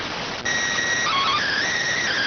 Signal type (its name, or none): SSTV (slow-scan television)